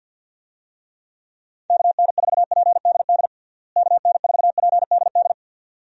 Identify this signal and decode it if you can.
Morse code, heard as a keyed tone, about 47 wpm, 700 Hz. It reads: KN4PDD KN4PDD
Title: USA Call Signs - 6 Characters - Repeated 2x - 50wpm